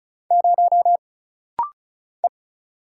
Morse code, heard as a keyed tone, about 34 wpm, 700 Hz. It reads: 0 E
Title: Single Letter-Number - Rapid-Fire 35wpm